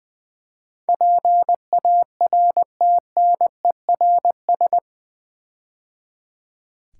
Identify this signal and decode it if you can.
Morse code, heard as a keyed tone, about 20 wpm, 700 Hz. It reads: PARTNERS
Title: Build A Word - 4 Letters At A Time - 11-Letter Words - 20wpm